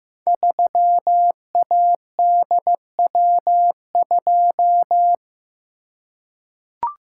Morse code, heard as a keyed tone, about 15 wpm, 700 Hz. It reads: 3ADW2
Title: Code Groups - 5 Letter:Numbers - 15wpm